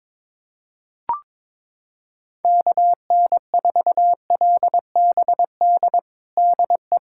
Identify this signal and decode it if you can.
Morse code, heard as a keyed tone, about 21 wpm, 700 Hz. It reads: KN4LBD DE
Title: USA Callsign DE Callsign - 6 Characters - 22wpm